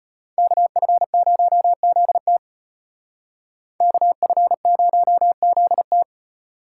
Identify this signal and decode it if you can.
Morse code, heard as a keyed tone, about 34 wpm, 700 Hz. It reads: KF0ZT KF0ZT